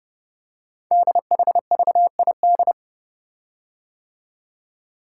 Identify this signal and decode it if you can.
Morse code, heard as a keyed tone, about 30 wpm, 690 Hz. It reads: DHVID
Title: Code Groups - 5 Letters - 30wpm